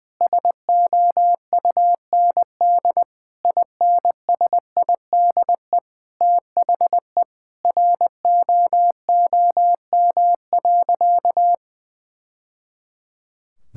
Morse code, heard as a keyed tone, about 21 wpm, 700 Hz. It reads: SOUND INSIDE THE ROOM.
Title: Sentences from Top 1000 Words - Review Part 6 - 20wpm